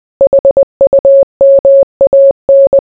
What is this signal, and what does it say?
Morse code, heard as a keyed tone, about 20 wpm, 550 Hz. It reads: HUMAN